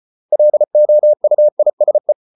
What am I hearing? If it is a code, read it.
Morse code, heard as a keyed tone, about 34 wpm, 600 Hz. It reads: LOUISE
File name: Ferie QRQ 34wpm